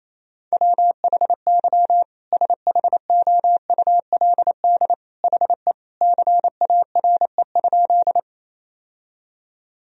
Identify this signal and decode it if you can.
Morse code, heard as a keyed tone, about 28 wpm, 700 Hz. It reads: WHY SHOULD HE CARE?
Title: Sentences from Top 1000 Words - Review Part 1 - 28wpm